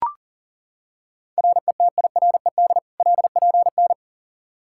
Morse code, heard as a keyed tone, about 40 wpm, 700 Hz. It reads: RETIRED LPN
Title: Practice QSO - Exchange 3 - More Info - 40wpm